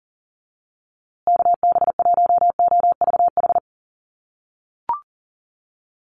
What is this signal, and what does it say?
Morse code, heard as a keyed tone, about 40 wpm, 700 Hz. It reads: KB1OVH
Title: USA Call Signs Repeated 2x 40wpm